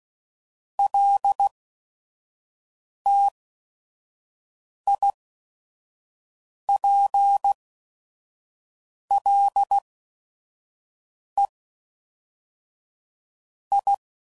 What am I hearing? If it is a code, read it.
Morse code, heard as a keyed tone, about 16 wpm, spaced as for 5 wpm, 800 Hz. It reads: LTIPLE I
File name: CW Code Groups @5WPM / Technical Subjects / AND Gate